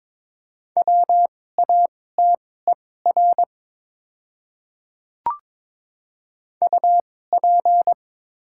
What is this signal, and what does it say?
Morse code, heard as a keyed tone, about 22 wpm, spaced as for 16 wpm, 700 Hz. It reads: WATER UP
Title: 2x Character Spacing - Top 100 Words - 22wpm